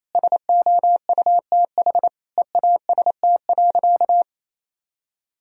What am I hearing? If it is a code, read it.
Morse code, heard as a keyed tone, about 28 wpm, 700 Hz. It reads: SOUTH EAST.
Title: Sentences from Top 1000 Words - Review Part 3 - 28wpm